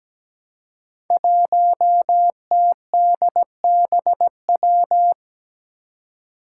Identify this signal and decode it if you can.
Morse code, heard as a keyed tone, about 17 wpm, 700 Hz. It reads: 1TDBW
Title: Five Challenging Letter-Number Combinations 17wpm